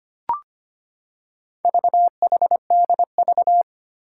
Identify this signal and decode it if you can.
Morse code, heard as a keyed tone, about 25 wpm, 700 Hz. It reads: VHDV
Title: Four Challenging Letter-Number Combinations 25wpm